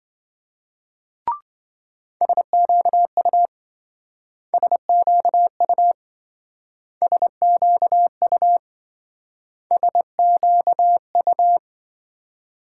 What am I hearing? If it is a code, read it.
Morse code, heard as a keyed tone, about 26 wpm, 700 Hz. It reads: SQU SQU SQU SQU